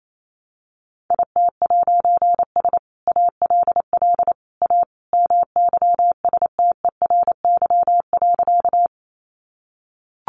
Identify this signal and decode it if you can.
Morse code, heard as a keyed tone, about 28 wpm, 700 Hz. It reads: IT'S ALL A MYSTERY.
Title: A Sound Of Thunder 28wpm